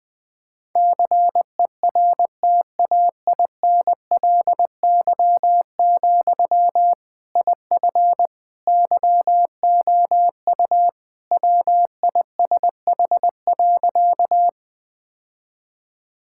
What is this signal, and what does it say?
Morse code, heard as a keyed tone, about 20 wpm, 700 Hz. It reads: CERTAINLY, IF YOU WISH.
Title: Sentences from Top 1600 Words 20wpm